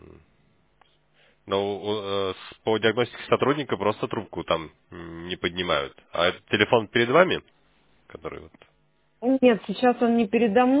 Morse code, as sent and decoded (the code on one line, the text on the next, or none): none